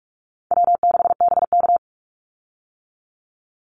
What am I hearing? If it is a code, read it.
Morse code, heard as a keyed tone, about 44 wpm, 700 Hz. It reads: W6BK
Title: Easy Call Signs - 45wpm